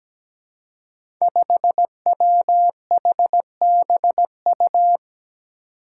Morse code, heard as a keyed tone, about 17 wpm, 700 Hz. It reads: 5WHBU
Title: Five Challenging Letter-Number Combinations 17wpm